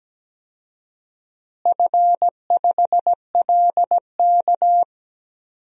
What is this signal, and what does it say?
Morse code, heard as a keyed tone, about 17 wpm, 700 Hz. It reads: F5LK
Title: DXCC-100 Call Signs - 4 Characters - Repeated 3x - 17wpm